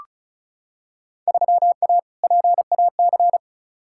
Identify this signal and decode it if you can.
Morse code, heard as a keyed tone, about 36 wpm, 700 Hz. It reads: 3A PAC
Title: Field Day Exchanges - 35wpm